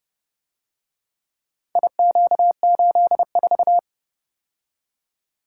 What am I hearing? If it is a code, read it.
Morse code, heard as a keyed tone, about 30 wpm, 700 Hz. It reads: IQ84